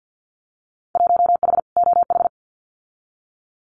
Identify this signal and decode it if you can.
Morse code, heard as a keyed tone, about 50 wpm, 700 Hz. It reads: 1H OH
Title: Winter Field Day Exchanges 50wpm